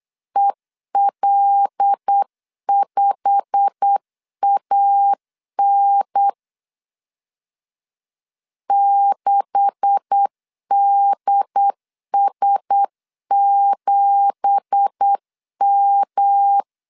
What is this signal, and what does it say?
Morse code, heard as a keyed tone, about 8 wpm, 800 Hz. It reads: EL5AN 6DS7M
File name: CW / DET / 1 9 1500C S A 8wpm 5637